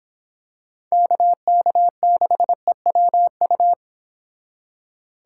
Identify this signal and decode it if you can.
Morse code, heard as a keyed tone, about 25 wpm, 700 Hz. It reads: KK6EWU